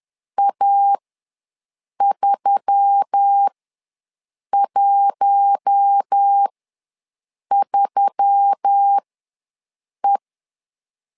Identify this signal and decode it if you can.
Morse code, heard as a keyed tone, about 10 wpm, spaced as for 6 wpm, 800 Hz. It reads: A313E
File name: LEZIONI / 4 Q1S3 ws 10wpm 2437